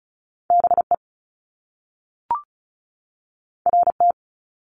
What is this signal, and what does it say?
Morse code, heard as a keyed tone, about 35 wpm, 700 Hz. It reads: BE RT
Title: Top Two Letter Combinations in Words 35wpm